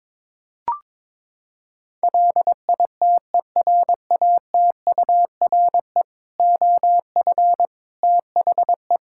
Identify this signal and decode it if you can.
Morse code, heard as a keyed tone, about 21 wpm, 700 Hz. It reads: LITERATURE OF THE
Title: Sentences from Top 1000 Words - Review Part 5 - 22wpm